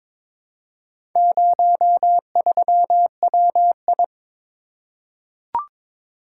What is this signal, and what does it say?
Morse code, heard as a keyed tone, about 23 wpm, 700 Hz. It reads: 03WI
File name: practice / 22wpm / Four Letter-Number Combinations - 22wpm